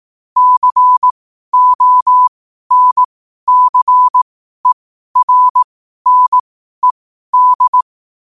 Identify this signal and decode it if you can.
Morse code, heard as a keyed tone, about 18 wpm, spaced as for 12 wpm, 1000 Hz. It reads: CONCERNED
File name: qotdfiles / 12wpm